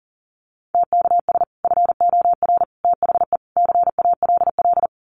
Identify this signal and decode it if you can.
Morse code, heard as a keyed tone, about 40 wpm, 700 Hz. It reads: TKS FOR THE CALL
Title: Common QSO Phrases 40wpm